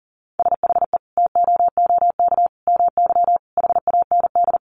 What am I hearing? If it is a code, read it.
Morse code, heard as a keyed tone, about 40 wpm, 700 Hz. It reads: SHE TOOK MY HAND
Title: Sentences from Top 1500 Words 40wpm